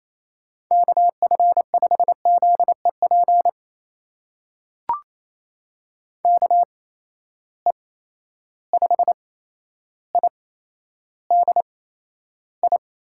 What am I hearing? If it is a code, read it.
Morse code, heard as a keyed tone, about 28 wpm, spaced as for 12 wpm, 700 Hz. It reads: KF5ZEP KE5IDI
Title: USA Call Signs - 6 Characters - 8x Character Spacing - 28wpm